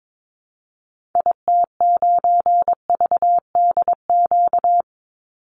Morse code, heard as a keyed tone, about 22 wpm, 700 Hz. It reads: IT9VDQ
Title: Call Signs - 22wpm